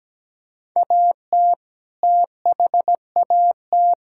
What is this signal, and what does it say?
Morse code, heard as a keyed tone, about 17 wpm, 700 Hz. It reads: AT THAT
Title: Sentences from Top 1000 Words - Review Part 5 - 17wpm